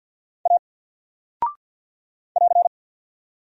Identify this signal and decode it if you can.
Morse code, heard as a keyed tone, about 48 wpm, 700 Hz. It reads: A <AR>
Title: Single Character (Letter:Number:Punctuation:Prosign) 50wpm